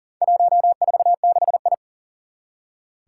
Morse code, heard as a keyed tone, about 40 wpm, 700 Hz. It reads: 146I
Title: Four Challenging Letter-Number Combinations 40wpm